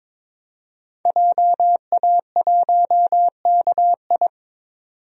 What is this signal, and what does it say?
Morse code, heard as a keyed tone, about 22 wpm, 700 Hz. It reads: JA1KI
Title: DXCC-100 Call Signs - 5 Characters - 22wpm